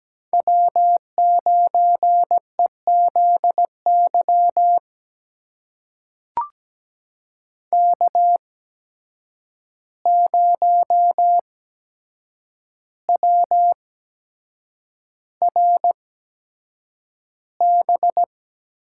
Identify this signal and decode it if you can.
Morse code, heard as a keyed tone, about 17 wpm, spaced as for 7 wpm, 700 Hz. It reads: W9EZY K0WRB
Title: USA Call Signs - 5 Characters - 8x Character Spacing - 17wpm